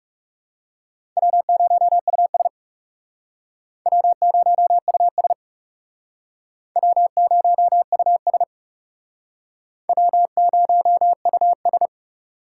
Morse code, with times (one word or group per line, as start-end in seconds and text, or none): 1.2-2.5 W0US
3.9-5.3 W0US
6.8-8.4 W0US
9.9-11.9 W0US